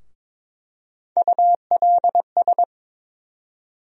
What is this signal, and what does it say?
Morse code, heard as a keyed tone, about 22 wpm, 700 Hz. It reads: ULS
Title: USA Call Sign Postfixes - 3 Letters 22wpm